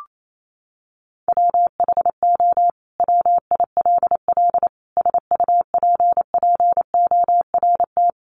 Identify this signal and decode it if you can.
Morse code, heard as a keyed tone, about 28 wpm, 700 Hz. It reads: WHO WILL SUPPORT
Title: Sentences from Top 800 Words 28wpm